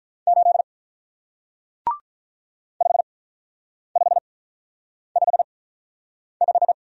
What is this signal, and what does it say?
Morse code, heard as a keyed tone, about 48 wpm, spaced as for 9 wpm, 700 Hz. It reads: 8 5555